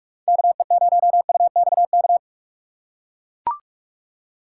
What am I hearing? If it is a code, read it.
Morse code, heard as a keyed tone, about 44 wpm, 700 Hz. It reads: KE0UXK
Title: USA Call Signs - 6 Characters - 2x Character Spacing - 45wpm